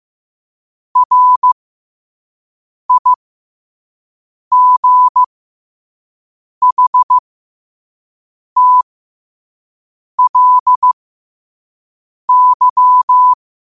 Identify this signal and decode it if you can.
Morse code, heard as a keyed tone, about 15 wpm, spaced as for 5 wpm, 1000 Hz. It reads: RIGHTLY
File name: qotdfiles / 5wpm